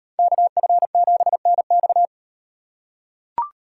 Morse code, heard as a keyed tone, about 38 wpm, 700 Hz. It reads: KF7NX